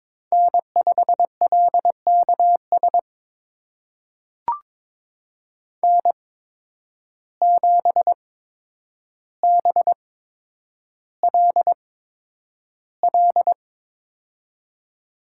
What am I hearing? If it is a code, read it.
Morse code, heard as a keyed tone, about 22 wpm, spaced as for 9 wpm, 700 Hz. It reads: N5LKS N7BLL
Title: USA Call Signs - 5 Characters - 8x Character Spacing - 22wpm